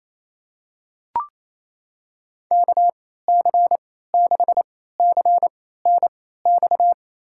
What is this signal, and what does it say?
Morse code, heard as a keyed tone, about 28 wpm, spaced as for 16 wpm, 700 Hz. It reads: KC6CNX
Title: USA Call Signs - 6 Characters - 3x Character Spacing - 28wpm